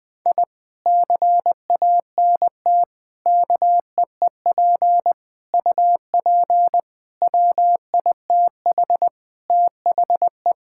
Morse code, heard as a keyed tone, about 19 wpm, 700 Hz. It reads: I CANT KEEP UP WITH THE
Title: Sentences with 2-Word Phrases - 20wpm